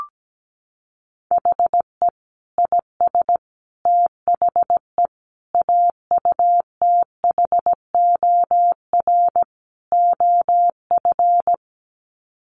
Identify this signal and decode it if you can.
Morse code, heard as a keyed tone, about 17 wpm, 700 Hz. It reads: HE IS THE AUTHOR OF
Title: Sets of 5 Words Encore2 17wpm